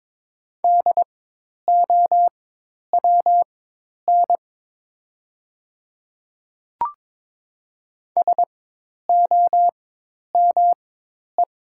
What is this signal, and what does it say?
Morse code, heard as a keyed tone, about 22 wpm, spaced as for 10 wpm, 700 Hz. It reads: DOWN SOME
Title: 4x Character Spacing - Top 100 Words - 22wpm